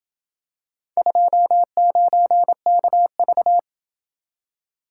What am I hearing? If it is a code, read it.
Morse code, heard as a keyed tone, about 27 wpm, 700 Hz. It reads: 29KV